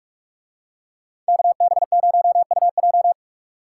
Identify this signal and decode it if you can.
Morse code, heard as a keyed tone, about 40 wpm, 700 Hz. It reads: KB0UJ